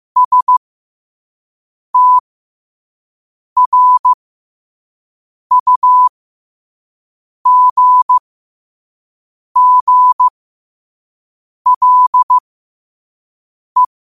Morse code, heard as a keyed tone, about 15 wpm, spaced as for 5 wpm, 1000 Hz. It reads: STRUGGLE
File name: qotdfiles / 5wpm